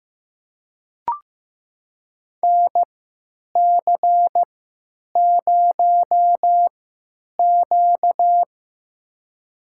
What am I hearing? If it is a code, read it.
Morse code, heard as a keyed tone, about 15 wpm, spaced as for 9 wpm, 700 Hz. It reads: NC0Q
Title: USA Call Signs - 4 Characters - 3x Character Spacing - 15wpm